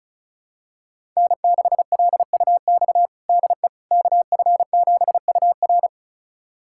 Morse code, heard as a keyed tone, about 36 wpm, 700 Hz. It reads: N6LUX DE KF7UR
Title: USA Callsign DE Callsign - 5 Characters - 35wpm